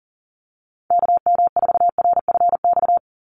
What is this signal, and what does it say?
Morse code, heard as a keyed tone, about 39 wpm, 700 Hz. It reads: KM4RFX
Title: USA Call Signs 40wpm